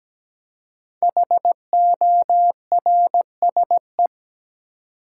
Morse code, heard as a keyed tone, about 17 wpm, 700 Hz. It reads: HORSE
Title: Five Letter Words - 17wpm